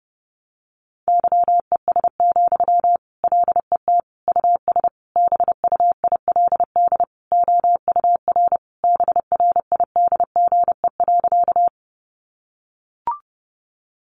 Morse code, heard as a keyed tone, about 30 wpm, 700 Hz. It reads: YES, LET US BUILD OUR BRIDGE.
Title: Sentences from Top 1200 Words 30wpm